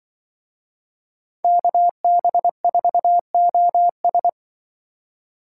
Morse code, heard as a keyed tone, about 24 wpm, 700 Hz. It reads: KB4OS
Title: Speed Racing - USA 2x2 Call Signs 20wpm